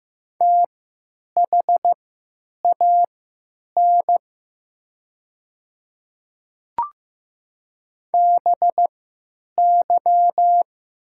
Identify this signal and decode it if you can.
Morse code, heard as a keyed tone, about 15 wpm, spaced as for 9 wpm, 700 Hz. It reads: THAN BY
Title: 3x Character Spacing - Top 200 Words - 15wpm